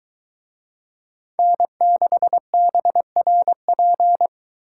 Morse code, heard as a keyed tone, about 22 wpm, 700 Hz. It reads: N6BRP